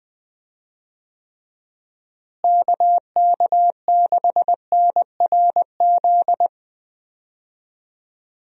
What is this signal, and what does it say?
Morse code, heard as a keyed tone, about 21 wpm, 700 Hz. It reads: KK6NRZ
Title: USA Call Signs - 6 Characters - Repeated 2x - 20wpm